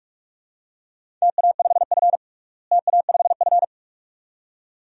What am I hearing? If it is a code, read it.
Morse code, heard as a keyed tone, about 42 wpm, 700 Hz. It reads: TA5F TA5F
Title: DXCC-100 Call Signs - 4 Characters - Repeated 2x - 45wpm